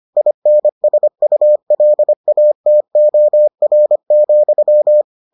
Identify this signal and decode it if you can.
Morse code, heard as a keyed tone, about 25 wpm, 600 Hz. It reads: INSULATOR,